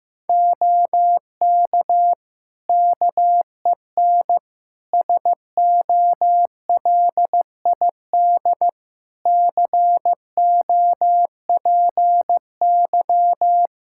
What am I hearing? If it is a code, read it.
Morse code, heard as a keyed tone, about 15 wpm, 700 Hz. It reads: OK KEN SOLID COPY
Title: Practice QSO - Exchange 3 - More Info - 15wpm